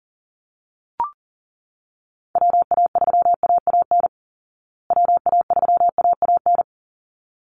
Morse code, heard as a keyed tone, about 40 wpm, 700 Hz. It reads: WA3AAN WA3AAN
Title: Call Signs Repeated 3x - 40wpm